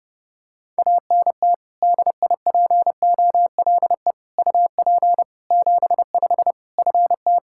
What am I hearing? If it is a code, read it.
Morse code, heard as a keyed tone, about 30 wpm, 700 Hz. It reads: ANT DIPOLE UP 75 FT
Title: K1USN SST QSO Practice 30wpm